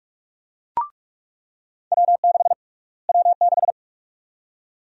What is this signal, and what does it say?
Morse code, heard as a keyed tone, about 43 wpm, 700 Hz. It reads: W6 W6
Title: USA Call Sign Prefixes - Repeated 2x - 45wpm